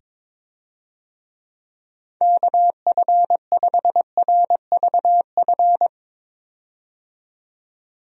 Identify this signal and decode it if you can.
Morse code, heard as a keyed tone, about 22 wpm, 700 Hz. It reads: KF5RVF
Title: USA Call Signs - 6 Characters - 22wpm